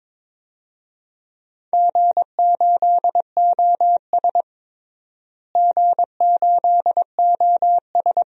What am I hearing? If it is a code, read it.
Morse code, heard as a keyed tone, about 22 wpm, 700 Hz. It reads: G8OS G8OS
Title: DXCC-100 Call Signs - 4 Characters - Repeated 2x - 22wpm